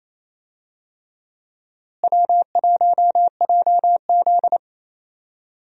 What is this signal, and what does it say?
Morse code, heard as a keyed tone, about 28 wpm, 700 Hz. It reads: W1JZ
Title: USA Call Signs - 4 Characters - Repeated 3x - 28wpm